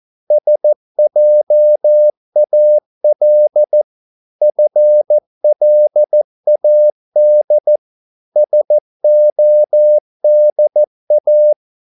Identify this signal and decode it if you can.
Morse code, heard as a keyed tone, about 14 wpm, 600 Hz. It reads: SJAL FLAD SODA
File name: Korte ord DK 14wpm